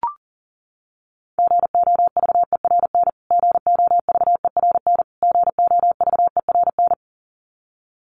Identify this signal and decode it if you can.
Morse code, heard as a keyed tone, about 40 wpm, 700 Hz. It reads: GOVERN GOVERN GOVERN
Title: Top 500 Words Repeated 40wpm